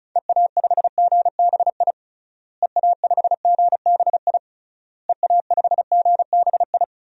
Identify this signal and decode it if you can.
Morse code, heard as a keyed tone, about 34 wpm, 700 Hz. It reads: EA5GBI EA5GBI EA5GBI
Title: DXCC-100 Call Signs - 6 Characters - Repeated 3x - 35wpm